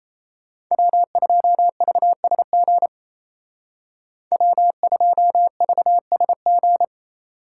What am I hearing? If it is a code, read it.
Morse code, heard as a keyed tone, about 36 wpm, 700 Hz. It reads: W2VSG W2VSG